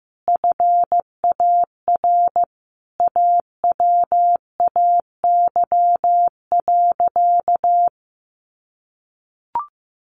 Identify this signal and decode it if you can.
Morse code, heard as a keyed tone, about 15 wpm, 700 Hz. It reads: FAR AWAY.
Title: Sentences from Top 600 Words 15wpm